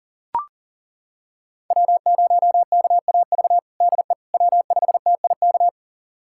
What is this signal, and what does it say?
Morse code, heard as a keyed tone, about 37 wpm, 700 Hz. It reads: W0KAV DE W5TIK
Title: USA Callsign DE Callsign - 5 Characters - 40wpm